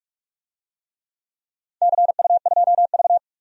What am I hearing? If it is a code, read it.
Morse code, heard as a keyed tone, about 41 wpm, 700 Hz. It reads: CU2V
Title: DXCC-100 Call Signs - 4 Characters - 45wpm